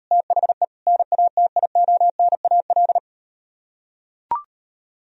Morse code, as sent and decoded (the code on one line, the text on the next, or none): - .... / -. .- - .. --- -. .- .-..
THE NATIONAL